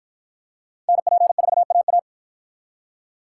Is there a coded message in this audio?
Yes: Morse code from 0.9 to 2.0 s, heard as a keyed tone, about 51 wpm, 700 Hz. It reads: NP4AA